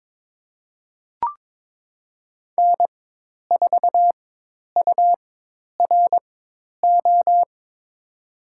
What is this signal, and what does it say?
Morse code, heard as a keyed tone, about 22 wpm, spaced as for 10 wpm, 700 Hz. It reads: N4URO
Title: USA Call Signs - 5 Characters - 4x Character Spacing - 22wpm